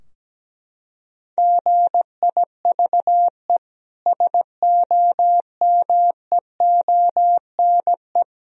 Morse code, heard as a keyed tone, about 17 wpm, 700 Hz. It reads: GIVE SOMEONE
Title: Sentences from Top 1000 Words - Review Part 4 - 17wpm